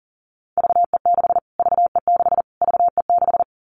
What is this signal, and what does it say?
Morse code, heard as a keyed tone, about 40 wpm, 700 Hz. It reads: VE6 VE6 VE6
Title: Callsign Prefixes for Top 25 DXCC - Repeated 3x - 40wpm